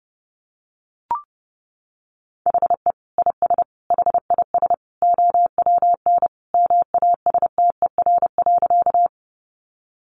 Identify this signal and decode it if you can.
Morse code, heard as a keyed tone, about 30 wpm, 700 Hz. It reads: HE IS HIS OWN MASTER.
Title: Sentences from Top 1300 Words 30wpm